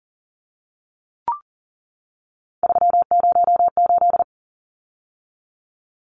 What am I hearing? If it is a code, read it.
Morse code, heard as a keyed tone, about 38 wpm, 700 Hz. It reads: 308
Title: Numbers - 3-Digit 40wpm